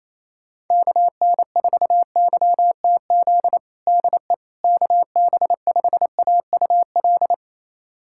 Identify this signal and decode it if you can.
Morse code, heard as a keyed tone, about 28 wpm, 700 Hz. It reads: KN4YTZ DE KB5AUL
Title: USA Callsign DE Callsign - 6 Characters - 28wpm